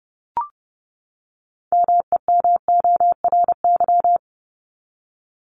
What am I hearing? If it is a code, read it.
Morse code, heard as a keyed tone, about 30 wpm, 700 Hz. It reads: MEMORY
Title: Top 1251-1500 Words 30wpm